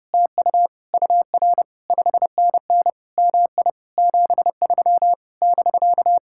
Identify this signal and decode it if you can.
Morse code, heard as a keyed tone, about 33 wpm, 700 Hz. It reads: TU UR 5NN MI 73 <BK>